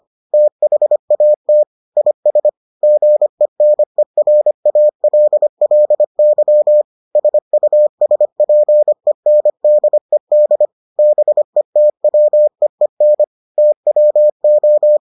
Morse code, heard as a keyed tone, about 25 wpm, 600 Hz. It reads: THAT IS GENERALLY SUSPENDED BETWEEN TWO